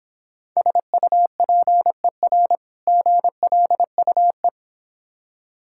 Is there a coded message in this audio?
Yes: Morse code from 0.6 to 4.5 s, heard as a keyed tone, about 26 wpm, 700 Hz. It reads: SUPER GLUE